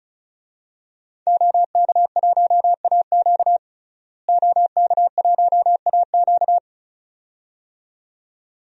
Morse code, heard as a keyed tone, about 36 wpm, 700 Hz. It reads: OK1AQ OK1AQ
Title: DXCC-100 Call Signs - 5 Characters - Repeated 2x - 35wpm